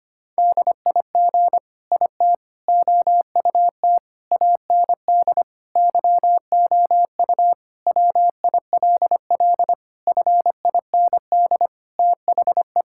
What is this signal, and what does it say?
Morse code, heard as a keyed tone, about 25 wpm, 700 Hz. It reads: DIG IT OUT AND YOU WILL FIND THE
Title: Sentences from Top 1600 Words 25wpm